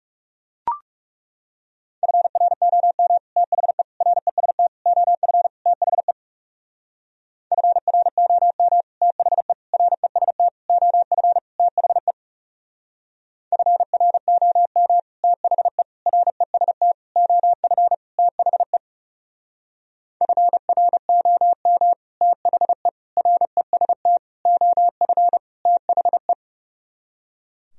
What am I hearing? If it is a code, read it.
Morse code, heard as a keyed tone, about 40 wpm, 700 Hz. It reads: FROM THE REST OF THE FROM THE REST OF THE FROM THE REST OF THE FROM THE REST OF THE